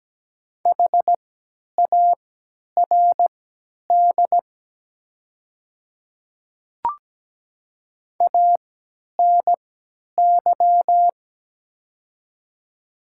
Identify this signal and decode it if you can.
Morse code, heard as a keyed tone, about 17 wpm, spaced as for 10 wpm, 700 Hz. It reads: HARD ANY
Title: 3x Character Spacing - Top 300 Words - 17wpm